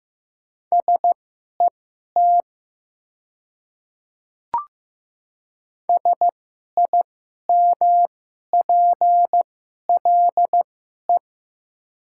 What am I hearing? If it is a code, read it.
Morse code, heard as a keyed tone, about 16 wpm, spaced as for 11 wpm, 700 Hz. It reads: SET SIMPLE
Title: 2x Character Spacing - Top 500 Words - 15wpm